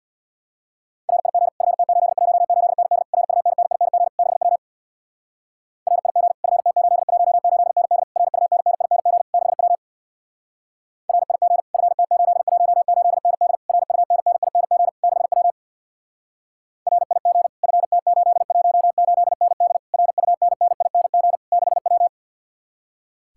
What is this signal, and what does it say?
Morse code, heard as a keyed tone, about 64 wpm, 700 Hz. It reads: RIG FT818ND RUNNING 6W RIG FT818ND RUNNING 6W RIG FT818ND RUNNING 6W RIG FT818ND RUNNING 6W